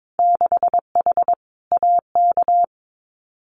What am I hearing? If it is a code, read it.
Morse code, heard as a keyed tone, about 22 wpm, 700 Hz. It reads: 6H AK
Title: Winter Field Day Exchanges Encore 22wpm